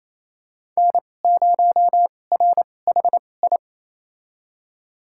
Morse code, heard as a keyed tone, about 28 wpm, spaced as for 20 wpm, 700 Hz. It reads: N0RHI
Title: USA Call Signs - 5 Characters - 2x Character Spacing - 28wpm